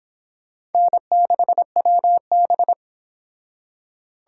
Morse code, heard as a keyed tone, about 28 wpm, 700 Hz. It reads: N6WB